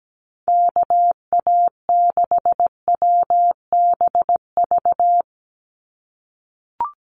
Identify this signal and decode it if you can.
Morse code, heard as a keyed tone, about 17 wpm, 700 Hz. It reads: KA6WBV
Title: USA Call Signs 17wpm